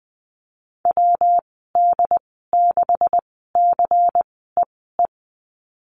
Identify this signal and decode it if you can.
Morse code, heard as a keyed tone, about 20 wpm, spaced as for 14 wpm, 700 Hz. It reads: WD6CEE
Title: 140 Lesson 23g - Letter f - Call Signs 20wpm